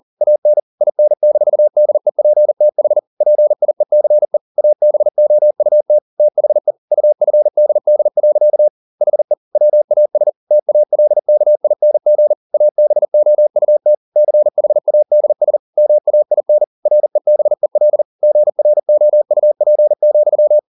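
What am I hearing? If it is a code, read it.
Morse code, heard as a keyed tone, about 40 wpm, 600 Hz. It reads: AN IN-DEPTH PIECE ABOUT THE UFDD. HE WAS TALKING ABOUT CHADS MAIN REBEL GROUP,
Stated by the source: Morse code at 40 wpm, practice text, CW med Gnister.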